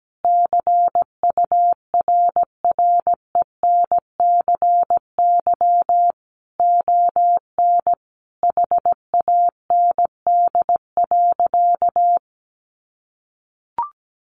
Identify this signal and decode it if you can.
Morse code, heard as a keyed tone, about 17 wpm, 700 Hz. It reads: CURRENCY ON HAND.
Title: Sentences from Top 1500 Words 17wpm